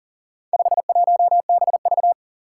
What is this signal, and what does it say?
Morse code, heard as a keyed tone, about 40 wpm, 700 Hz. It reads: 51BV
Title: Four Challenging Letter-Number Combinations 40wpm